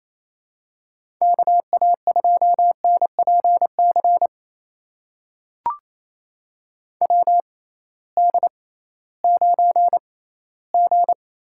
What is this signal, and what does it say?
Morse code, heard as a keyed tone, about 28 wpm, spaced as for 16 wpm, 700 Hz. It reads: KA2NPC WD9G
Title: USA Call Signs - 6 Characters - 6x Character Spacing - 28wpm